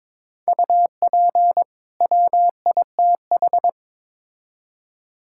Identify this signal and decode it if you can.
Morse code, heard as a keyed tone, about 22 wpm, 700 Hz. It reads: UP WITH
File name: Build A Phrase - 2-Word Phrases - 22wpm